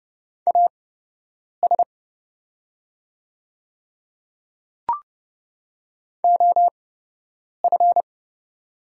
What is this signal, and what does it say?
Morse code, heard as a keyed tone, about 30 wpm, spaced as for 8 wpm, 700 Hz. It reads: AS OF
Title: 8x Character Spacing - Top 300 Words - 30wpm